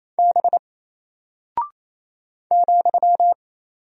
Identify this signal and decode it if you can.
Morse code, heard as a keyed tone, about 28 wpm, 700 Hz. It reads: B ,
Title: Single Character (Letter:Number:Punctuation:Prosign) Alt 28wpm